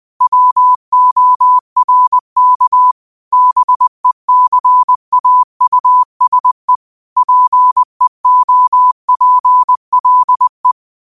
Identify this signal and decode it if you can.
Morse code, heard as a keyed tone, about 20 wpm, 1000 Hz. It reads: WORK BECAUSE PEOPLE
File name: qotdfiles / 20wpm